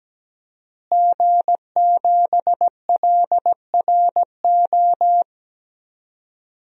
Morse code, heard as a keyed tone, about 17 wpm, 700 Hz. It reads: G7LRO
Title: DXCC-100 Call Signs - 5 Characters - Repeated 3x - 17wpm